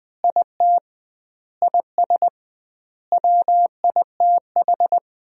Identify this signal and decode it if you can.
Morse code, heard as a keyed tone, about 20 wpm, 700 Hz. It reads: IT IS WITH
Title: 2x Word-Spacing - Sentences from Top 100 Words - 20wpm